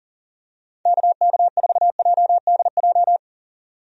Morse code, heard as a keyed tone, about 40 wpm, 700 Hz. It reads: KK4JDJ
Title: USA Call Signs - 6 Characters - 40wpm